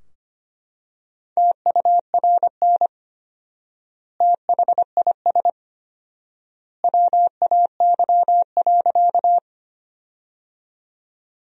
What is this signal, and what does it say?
Morse code, heard as a keyed tone, about 25 wpm, 700 Hz. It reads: TURN THIS WAY.
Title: 4x Word-Spacing - Sentences from Top 200 Words - 25wpm